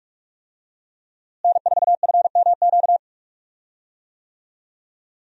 Morse code, heard as a keyed tone, about 46 wpm, 700 Hz. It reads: N4FMQ
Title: USA Call Signs - 5 Characters - Repeated 3x - 45wpm